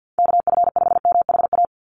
Morse code, heard as a keyed tone, about 50 wpm, 700 Hz. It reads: KF5MHA
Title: USA Call Signs 50wpm